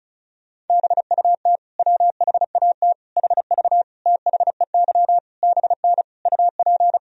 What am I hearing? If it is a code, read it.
Morse code, heard as a keyed tone, about 36 wpm, 700 Hz. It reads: BUT WHAT HV THEY BN UP
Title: Sentences from Top 100 Words and Abbreviations 35wpm